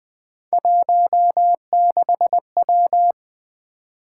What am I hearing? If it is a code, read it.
Morse code, heard as a keyed tone, about 20 wpm, 700 Hz. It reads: 16W
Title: Three Letter-Number Combinations - 20wpm